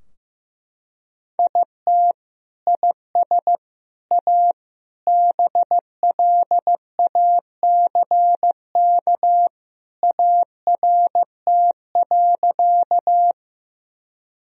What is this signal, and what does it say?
Morse code, heard as a keyed tone, about 15 wpm, 700 Hz. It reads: IT IS A BLACK ART.
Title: Sentences from Top 1000 Words - Review Part 3 - 15wpm